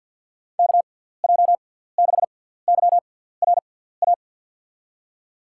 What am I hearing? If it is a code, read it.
Morse code, heard as a keyed tone, about 53 wpm, 700 Hz. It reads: K J 6 Y R A
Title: USA Call Signs - 6 Characters - 6x Character Spacing - 50wpm